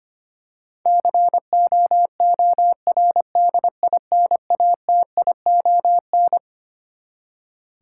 Morse code, heard as a keyed tone, about 25 wpm, 700 Hz. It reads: COORDINATION
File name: Twelve Letter Words - 25wpm